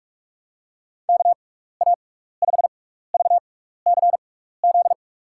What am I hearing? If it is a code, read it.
Morse code, heard as a keyed tone, about 46 wpm, spaced as for 15 wpm, 690 Hz. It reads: KA5VCZ
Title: USA Call Signs - 6x Character Spacing - 45wpm